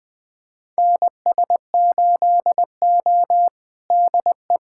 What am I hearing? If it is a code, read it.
Morse code, heard as a keyed tone, about 20 wpm, 700 Hz. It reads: NS8O DE
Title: USA Callsign DE Callsign - 4 Characters - 20wpm